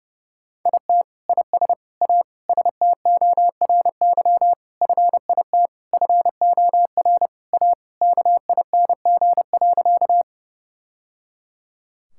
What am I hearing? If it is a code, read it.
Morse code, heard as a keyed tone, about 29 wpm, 700 Hz. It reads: IT IS A STORY FIT FOR A KING.
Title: Sentences from Top 1000 Words - Review Part 5 - 30wpm